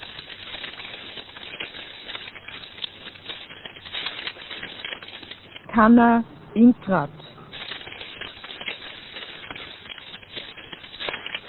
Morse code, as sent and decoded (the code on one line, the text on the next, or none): none